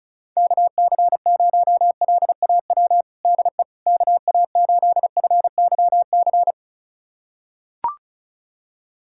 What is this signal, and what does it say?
Morse code, heard as a keyed tone, about 34 wpm, 690 Hz. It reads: KC0LAW DE KA8FYC
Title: USA Callsign DE Callsign - 6 Characters - 35wpm